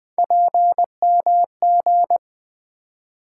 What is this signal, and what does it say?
Morse code, heard as a keyed tone, about 20 wpm, 700 Hz. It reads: PMG
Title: Three Letter-Number Combinations - 20wpm